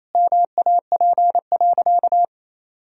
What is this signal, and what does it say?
Morse code, heard as a keyed tone, about 28 wpm, 700 Hz. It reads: MAP.
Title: Sentences from Top 1000 Words - Review Part 2 - 28wpm